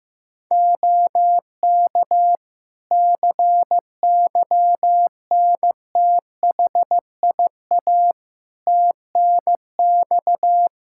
Morse code, heard as a keyed tone, about 15 wpm, 700 Hz. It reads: OK CYNTHIA TNX
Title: K1USN SST QSO Practice 15wpm